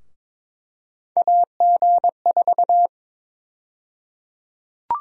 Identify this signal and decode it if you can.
Morse code, heard as a keyed tone, about 21 wpm, 700 Hz. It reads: AG4
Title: USA Call Sign Prefixes - 22wpm